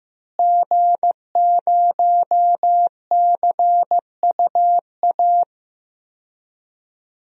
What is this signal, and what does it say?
Morse code, heard as a keyed tone, about 15 wpm, 700 Hz. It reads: G0CUA
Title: DXCC-100 Call Signs - 5 Characters - Repeated 3x - 15wpm